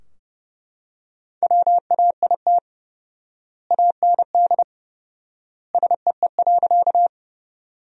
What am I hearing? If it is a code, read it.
Morse code, heard as a keyed tone, about 30 wpm, 700 Hz. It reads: WAIT AND SEE.
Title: 4x Word-Spacing - Sentences from Top 500 Words - 30wpm